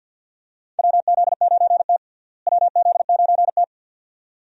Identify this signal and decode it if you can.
Morse code, heard as a keyed tone, about 50 wpm, 700 Hz. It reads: WZ9T WZ9T
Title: USA Call Signs - 4 Characters - Repeated 2x - 50wpm